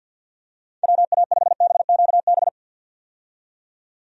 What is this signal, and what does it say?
Morse code, heard as a keyed tone, about 49 wpm, 700 Hz. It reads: WA5BQB
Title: USA Call Signs - 6 Characters - 50wpm